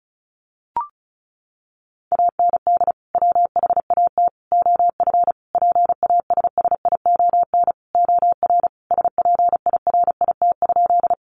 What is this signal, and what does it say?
Morse code, heard as a keyed tone, about 34 wpm, 700 Hz. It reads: AND WHAT OF PASSION OR SPIRIT?
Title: Sentences from Top 1400 Words 35wpm